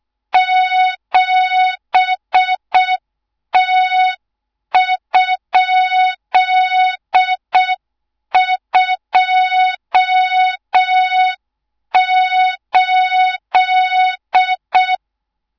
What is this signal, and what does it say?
Morse code, heard as a keyed tone, about 6 wpm, 3700 Hz, with noise. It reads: 7T?28
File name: Morze-HA2MN / hang / 24 GYAKORLAT